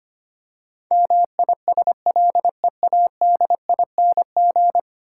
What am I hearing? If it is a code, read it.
Morse code, heard as a keyed tone, about 25 wpm, 700 Hz. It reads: MISLEADING
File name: 10 Letter Words - First 3 Letters Given - 25wpm